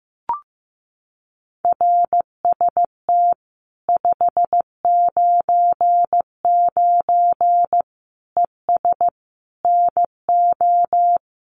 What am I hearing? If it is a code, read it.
Morse code, heard as a keyed tone, about 15 wpm, 700 Hz. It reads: RST 599 ES NO
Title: Common QSO Phrases 15wpm